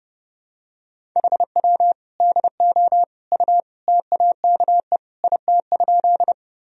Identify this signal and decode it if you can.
Morse code, heard as a keyed tone, about 30 wpm, 700 Hz. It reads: HW DO U TAKE IT?
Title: Sentences from Top 200 Words and Abbreviations 30wpm